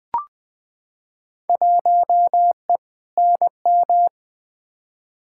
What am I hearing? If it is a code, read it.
Morse code, heard as a keyed tone, about 20 wpm, 700 Hz. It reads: 1E NM
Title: Field Day Exchanges - 20wpm